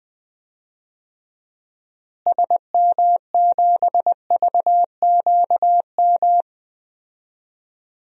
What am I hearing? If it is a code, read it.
Morse code, heard as a keyed tone, about 20 wpm, 700 Hz. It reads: SM7VQM
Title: DXCC-100 Call Signs - 6 Characters - Repeated 2x - 20wpm